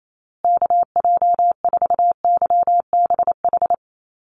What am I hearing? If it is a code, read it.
Morse code, heard as a keyed tone, about 28 wpm, 700 Hz. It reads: KJ4YBH
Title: USA Call Signs Repeated 2x 28wpm